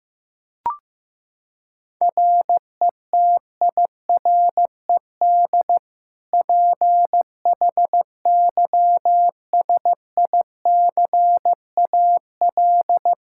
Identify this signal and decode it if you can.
Morse code, heard as a keyed tone, about 16 wpm, 700 Hz. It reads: RETIRED PHYSICAL
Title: Practice QSO - Exchange 3 - More Info - 15wpm